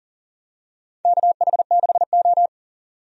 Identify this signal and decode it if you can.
Morse code, heard as a keyed tone, about 38 wpm, 700 Hz. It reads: KH6O
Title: DXCC-100 Call Signs - 4 Characters - 40wpm